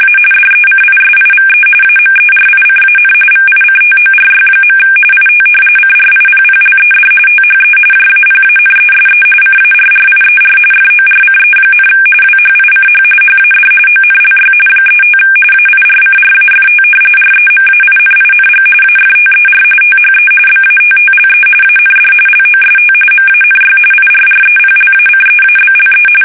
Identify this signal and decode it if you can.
RTTY, about 45.45 baud. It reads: A DECISIVE AMERICAN VICTORY. IT CRIPPLED THE JAPANESE CARRIER FORCE, WHICH HAD ONLY 35 AIRCRAFT REMAINING AFTER THE BATTLE AND LOST MANY IRREPLACEABLE
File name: fsk850 / battleshipnj